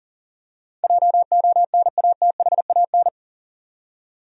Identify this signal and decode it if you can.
Morse code, heard as a keyed tone, about 40 wpm, 700 Hz. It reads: JONATHAN
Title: USA Common Names - 40wpm